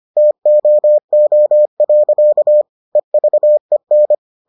Morse code, heard as a keyed tone, about 25 wpm, 600 Hz. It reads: TOO. EVEN